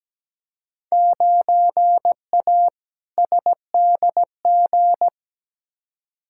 Morse code, heard as a keyed tone, about 17 wpm, 700 Hz. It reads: 9A SDG
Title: Field Day Exchanges Encore - 17wpm